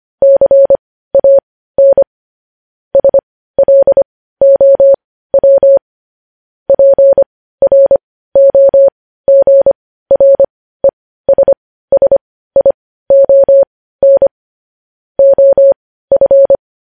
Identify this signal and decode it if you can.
Morse code, heard as a keyed tone, about 25 wpm, spaced as for 15 wpm, 550 Hz. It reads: CAN SLOW PROGRESSION OF
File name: podcast / News headlines-1x-25x15